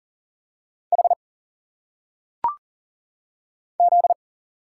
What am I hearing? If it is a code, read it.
Morse code, heard as a keyed tone, about 40 wpm, 700 Hz. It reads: H Z
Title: Single Letters - Phonetic Pronunciation 40wpm